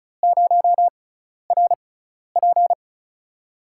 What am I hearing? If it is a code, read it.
Morse code, heard as a keyed tone, about 36 wpm, spaced as for 12 wpm, 700 Hz. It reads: 0RP
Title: USA Call Signs - 4 Characters - 6x Character Spacing - 35wpm